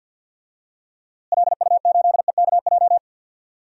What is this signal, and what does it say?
Morse code, heard as a keyed tone, about 52 wpm, 700 Hz. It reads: LU8EKJ
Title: DXCC-100 Call Signs - 6 Characters - Repeated 3x - 50wpm